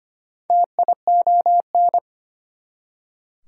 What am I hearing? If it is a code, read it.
Morse code, heard as a keyed tone, about 25 wpm, 700 Hz. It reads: TION
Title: Prefixes and Suffixes 25wpm